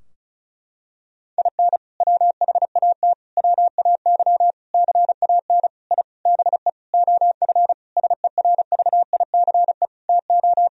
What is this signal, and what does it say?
Morse code, heard as a keyed tone, about 36 wpm, 700 Hz. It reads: IN WHAT WAY CAN I BE OF SERVICE TO
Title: Sentences from Top 1000 Words - Review Part 4 - 35wpm